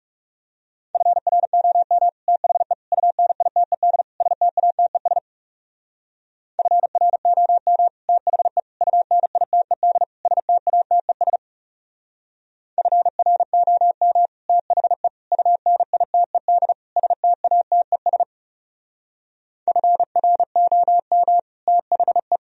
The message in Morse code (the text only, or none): FROM THE UNITED STATES FROM THE UNITED STATES FROM THE UNITED STATES FROM THE